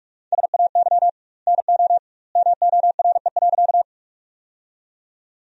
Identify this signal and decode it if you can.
Morse code, heard as a keyed tone, about 46 wpm, 700 Hz. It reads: SAY NO MORE.
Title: 2x Word-Spacing - Sentences from Top 200 Words - 45wpm